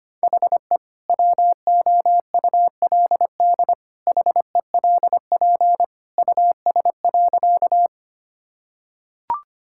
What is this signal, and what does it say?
Morse code, heard as a keyed tone, about 25 wpm, 700 Hz. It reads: HE WOULD HELP US.
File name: practice / 25wpm / Sentences with 2-Word Phrases - 25wpm